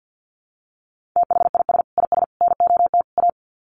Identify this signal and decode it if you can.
Morse code, heard as a keyed tone, about 50 wpm, 700 Hz. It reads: THIS IS NOT A
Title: Sets of 4 Words 50wpm